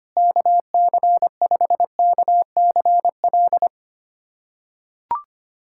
Morse code, heard as a keyed tone, about 25 wpm, 700 Hz. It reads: KC5KCL
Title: USA Callsign DE Callsign 25wpm